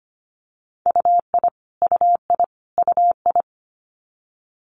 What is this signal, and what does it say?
Morse code, heard as a keyed tone, about 25 wpm, 700 Hz. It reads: UI UI UI
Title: Multi-letter Phonemes Repeated 3x 25wpm